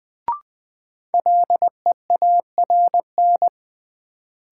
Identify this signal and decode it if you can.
Morse code, heard as a keyed tone, about 20 wpm, 700 Hz. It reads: LEARN
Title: Sentences from Top 300 Words - New Word Drill - 20wpm